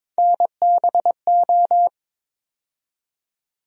Morse code, heard as a keyed tone, about 23 wpm, 700 Hz. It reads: NBO
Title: USA Call Sign Postfixes - 3 Letters 22wpm